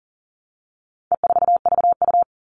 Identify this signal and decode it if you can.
Morse code, heard as a keyed tone, about 40 wpm, 700 Hz. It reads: E4VU
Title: All Cut Numbers - 4-Digit 40wpm